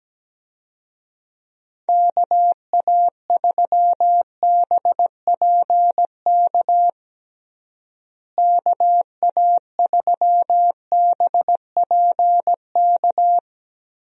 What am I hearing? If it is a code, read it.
Morse code, heard as a keyed tone, about 17 wpm, 700 Hz. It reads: KA3BPK KA3BPK
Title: USA Call Signs - 6 Characters - Repeated 3x - 17wpm